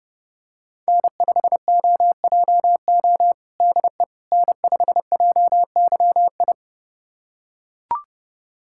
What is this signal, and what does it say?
Morse code, heard as a keyed tone, about 30 wpm, 700 Hz. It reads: N5OJO DE N5JYI
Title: USA Callsign DE Callsign - 5 Characters - 30wpm